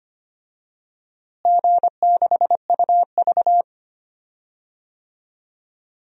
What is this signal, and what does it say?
Morse code, heard as a keyed tone, about 25 wpm, 700 Hz. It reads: G6UV
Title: DXCC-100 Call Signs - 4 Characters - Repeated 2x - 25wpm